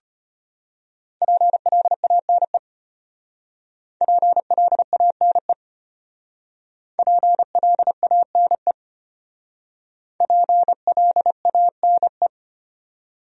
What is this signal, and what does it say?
Morse code, heard as a keyed tone, about 32 wpm, 700 Hz. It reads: PLANE PLANE PLANE PLANE